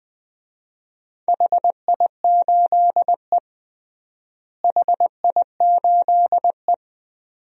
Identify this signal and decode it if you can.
Morse code, heard as a keyed tone, about 20 wpm, 700 Hz. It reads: HI8E HI8E
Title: DXCC-100 Call Signs - 4 Characters - Repeated 2x - 20wpm